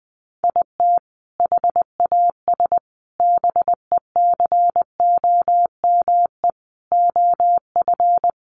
Morse code, heard as a keyed tone, about 20 wpm, 700 Hz. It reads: IT HAS BECOME OF
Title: Sentences from Top 1200 Words 20wpm